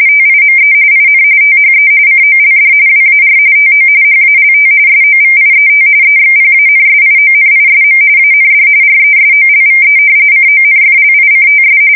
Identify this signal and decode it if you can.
RTTY, about 45.45 baud. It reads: TIME IN THE COLD WAR TO HALT SOVIET EXPANSIONISM AND RESOLVE CONFLICTS
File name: fsk170 / battleshipnj